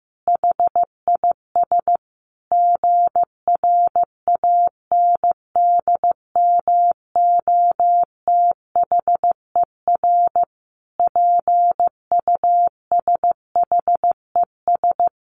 Morse code, heard as a keyed tone, about 14 wpm, 700 Hz. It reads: HIS GRANDMOTHER PUSHES
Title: Additional Easy Phrases 15wpm